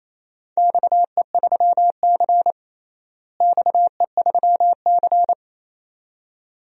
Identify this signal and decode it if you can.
Morse code, heard as a keyed tone, about 28 wpm, 700 Hz. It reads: XE3C XE3C
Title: DXCC-100 Call Signs - 4 Characters - Repeated 2x - 28wpm